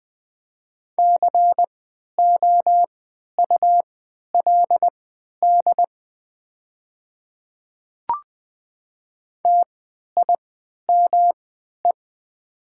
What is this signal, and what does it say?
Morse code, heard as a keyed tone, about 20 wpm, spaced as for 11 wpm, 700 Hz. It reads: COULD TIME
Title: 3x Character Spacing - Top 100 Words - 20wpm